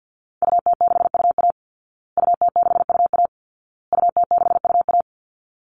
Morse code, heard as a keyed tone, about 50 wpm, 700 Hz. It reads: UT6UA UT6UA UT6UA
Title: Call Signs Repeated 2x - 50wpm